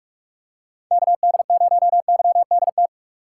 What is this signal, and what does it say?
Morse code, heard as a keyed tone, about 46 wpm, 700 Hz. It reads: KD0YDT
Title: USA Call Signs - 6 Characters - 3x Character Spacing - 45wpm